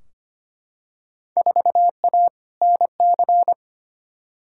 Morse code, heard as a keyed tone, about 25 wpm, 700 Hz. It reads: 4A NC
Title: Field Day Exchanges - 25wpm